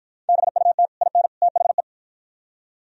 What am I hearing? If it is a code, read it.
Morse code, heard as a keyed tone, about 48 wpm, 700 Hz. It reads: BUT IN THE